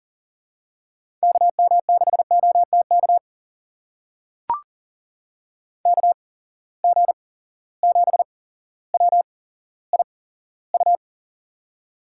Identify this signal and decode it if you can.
Morse code, heard as a keyed tone, about 37 wpm, spaced as for 17 wpm, 700 Hz. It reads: KM6OTK KG7WIU
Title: USA Call Signs - 6 Characters - 8x Character Spacing - 40wpm